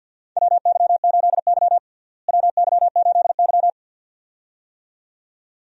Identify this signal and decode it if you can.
Morse code, heard as a keyed tone, about 52 wpm, 700 Hz. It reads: WY8Y WY8Y
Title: USA Call Signs - 4 Characters - Repeated 2x - 50wpm